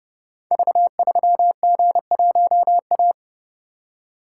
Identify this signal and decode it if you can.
Morse code, heard as a keyed tone, about 30 wpm, 700 Hz. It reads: V3G1A
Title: Code Groups - 5 Letter:Numbers - 30wpm